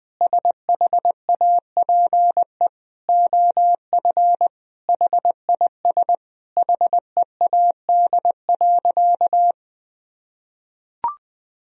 Morse code, heard as a keyed tone, about 19 wpm, 700 Hz. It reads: SHAPE OF HIS HEAD.
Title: Sentences from Top 1000 Words - Review Part 3 - 20wpm